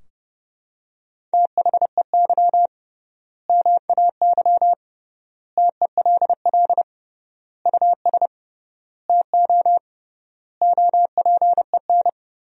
Morse code, heard as a keyed tone, about 32 wpm, 700 Hz. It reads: THEY MAY TELL US TO OPEN
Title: 3x Word-Spacing - Sentences from Top 300 Words - 30wpm